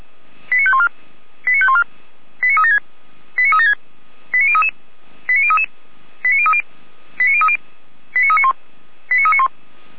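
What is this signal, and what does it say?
Five-tone selective call (ZVEI), heard as quick runs of five tones, tones of about 70 ms. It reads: 89514 89514 89376 89376 89033 89033 89030 89030 89391 89391
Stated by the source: Five-tone ZVEI selective call received on an SDR.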